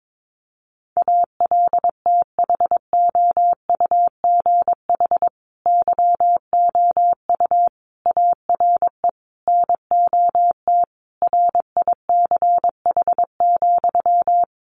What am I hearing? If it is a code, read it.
Morse code, heard as a keyed tone, about 22 wpm, 700 Hz. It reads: ALTHOUGH YOU ARE NOT RICH,
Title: Sentences from Top 1100 Words 22wpm